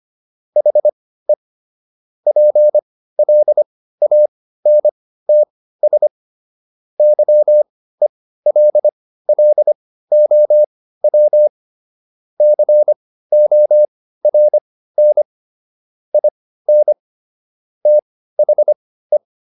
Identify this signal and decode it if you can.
Morse code, heard as a keyed tone, about 25 wpm, spaced as for 15 wpm, 600 Hz. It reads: HE PLANTS YELLOW CORN IN THE